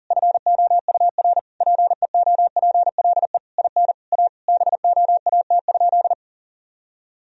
Morse code, heard as a keyed tone, about 40 wpm, 700 Hz. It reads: FOUR PEOPLE IN A BOAT?
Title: Sentences from Top 1000 Words - Review Part 3 - 40wpm